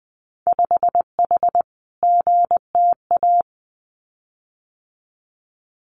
Morse code, heard as a keyed tone, about 20 wpm, 700 Hz. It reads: 5H GTA
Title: Winter Field Day Exchanges Encore 20wpm